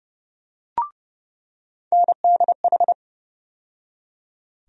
Morse code, heard as a keyed tone, about 32 wpm, 700 Hz. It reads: NDH
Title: USA Call Sign Postfixes - 3 Letters 30wpm